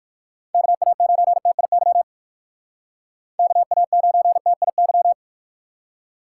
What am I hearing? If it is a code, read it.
Morse code, heard as a keyed tone, about 58 wpm, 700 Hz. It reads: KA9TIY KA9TIY